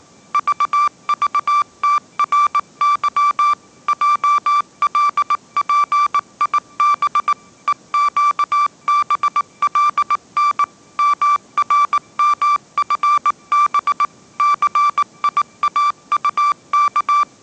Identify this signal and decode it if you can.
Morse code, heard as a keyed tone, about 19 wpm, 1170 Hz. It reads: VVTRY JLPIB EQBLN MRMFB CIAUK
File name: 20wpm